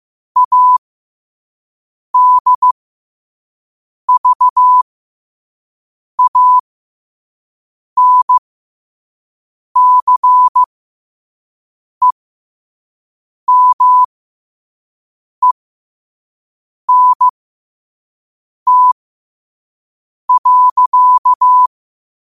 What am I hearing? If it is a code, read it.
Morse code, heard as a keyed tone, about 15 wpm, spaced as for 5 wpm, 1000 Hz. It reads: ADVANCEMENT.
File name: qotdfiles / 5wpm